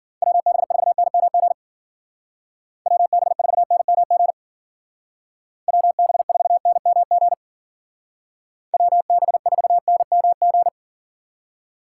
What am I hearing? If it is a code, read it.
Morse code, heard as a keyed tone, about 48 wpm, 700 Hz. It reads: WB4NMG WB4NMG WB4NMG WB4NMG
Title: Speed Racing - USA 2x3 Call Signs 40wpm